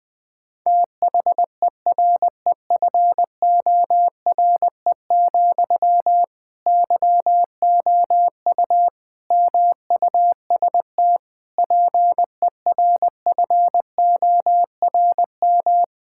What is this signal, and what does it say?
Morse code, heard as a keyed tone, about 20 wpm, 700 Hz. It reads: THEREFORE, YOU MUST PERFORM
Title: Sentences from Top 1000 Words - Review Part 5 - 20wpm